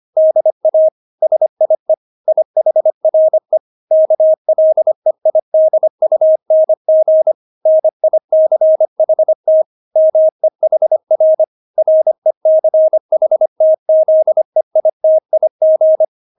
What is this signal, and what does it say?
Morse code, heard as a keyed tone, about 25 wpm, 630 Hz. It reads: DA SIE IHRE KLEIDUNG NICHT MEHR RECHTZEITIG
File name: Witz6 25wpm